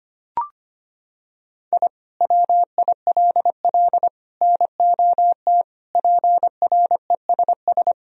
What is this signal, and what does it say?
Morse code, heard as a keyed tone, about 25 wpm, 700 Hz. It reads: I WILL NOT PRESS
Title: Sentences from Top 1000 Words - Review Part 1 - 25wpm